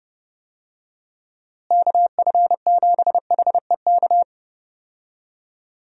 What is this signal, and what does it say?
Morse code, heard as a keyed tone, about 32 wpm, 700 Hz. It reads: KF7HEK
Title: USA Call Signs - 6 Characters - Repeated 3x - 30wpm